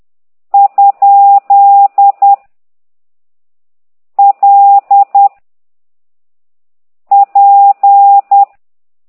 Morse code, heard as a keyed tone, about 10 wpm, 800 Hz. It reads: ? L P
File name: ct2idw / exerc8